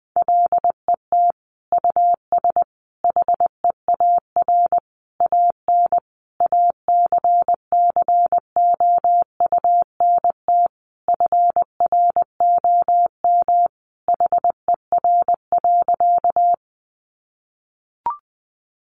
Morse code, heard as a keyed tone, about 20 wpm, 700 Hz. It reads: LET US HEAR AN ACCOUNT FROM HER.
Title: Sentences from Top 900 Words 20wpm